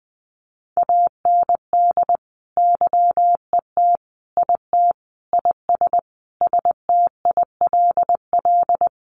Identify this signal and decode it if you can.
Morse code, heard as a keyed tone, about 19 wpm, 700 Hz. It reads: AND YET IT IS STILL
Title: Sentences from Top 1500 Words 20wpm